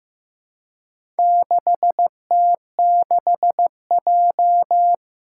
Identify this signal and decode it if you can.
Morse code, heard as a keyed tone, about 15 wpm, 700 Hz. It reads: 6T6J
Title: Four Challenging Letter-Number Combinations 15wpm